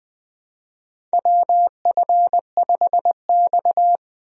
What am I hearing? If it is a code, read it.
Morse code, heard as a keyed tone, about 20 wpm, 700 Hz. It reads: WF5X
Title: USA Call Signs - 4 Characters - Repeated 3x - 20wpm